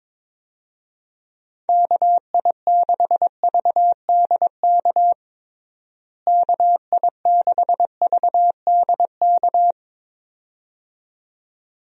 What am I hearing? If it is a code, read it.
Morse code, heard as a keyed tone, about 22 wpm, 700 Hz. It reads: KI6VDK KI6VDK
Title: USA Call Signs - 6 Characters - Repeated 2x - 22wpm